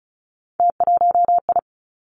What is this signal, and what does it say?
Morse code, heard as a keyed tone, about 36 wpm, 700 Hz. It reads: T1I
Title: Three Challenging Letter-Number Combinations 35wpm